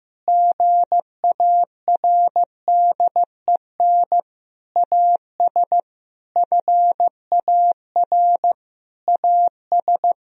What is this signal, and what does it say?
Morse code, heard as a keyed tone, about 15 wpm, 700 Hz. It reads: GARDEN AS FAR AS
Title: Sentences from Top 1000 Words - Review Part 4 - 15wpm